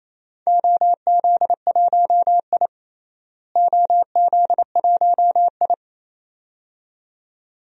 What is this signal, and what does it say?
Morse code, heard as a keyed tone, about 28 wpm, 700 Hz. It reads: OZ1I OZ1I
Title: DXCC-100 Call Signs - 4 Characters - Repeated 2x - 28wpm